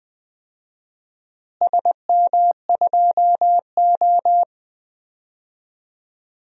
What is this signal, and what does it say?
Morse code, heard as a keyed tone, about 20 wpm, 700 Hz. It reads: SM2O
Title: DXCC-100 Call Signs - 4 Characters - Repeated 2x - 20wpm